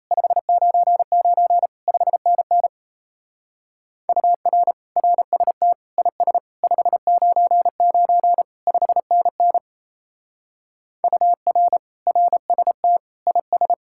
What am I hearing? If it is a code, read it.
Morse code, heard as a keyed tone, about 33 wpm, 700 Hz. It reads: 599 5NN UR RST IS 599 5NN UR RST IS
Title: Speed-Racing Common QSO Phrases - 28wpm